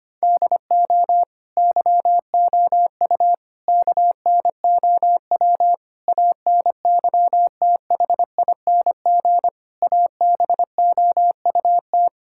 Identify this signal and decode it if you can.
Morse code, heard as a keyed tone, about 25 wpm, 700 Hz. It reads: DO YOU KNOW ANYTHING ABOUT
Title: Sentences from Top 1600 Words 25wpm